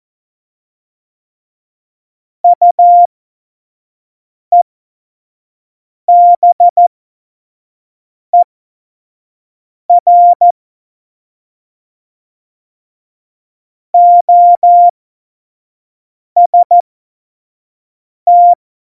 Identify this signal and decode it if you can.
Morse code, heard as a keyed tone, about 14 wpm, spaced as for 5 wpm, 700 Hz. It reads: UEBER OST
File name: Morsekurs / Witze / Witz 21 April 2020, 25 F70